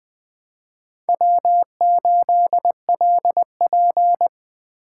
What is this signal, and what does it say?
Morse code, heard as a keyed tone, about 20 wpm, 700 Hz. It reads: W8LP
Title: Speed Racing - USA 1x2 Call Signs 17wpm